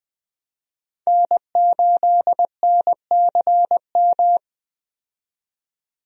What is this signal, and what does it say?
Morse code, heard as a keyed tone, about 20 wpm, 700 Hz. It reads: N8NCM
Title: USA Call Signs - 5 Characters - 20wpm